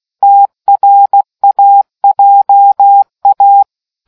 Morse code, heard as a keyed tone, about 16 wpm, 800 Hz. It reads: TRAJA